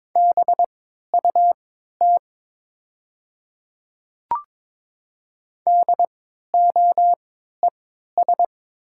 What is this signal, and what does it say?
Morse code, heard as a keyed tone, about 22 wpm, spaced as for 12 wpm, 700 Hz. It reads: BUT DOES
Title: 3x Character Spacing - Top 500 Words - 22wpm